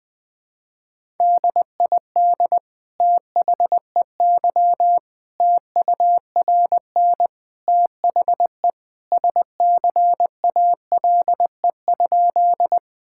Morse code, heard as a keyed tone, about 20 wpm, 700 Hz. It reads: DID THEY TURN THE SCALE?
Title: Sentences from Top 1000 Words - Review Part 5 - 20wpm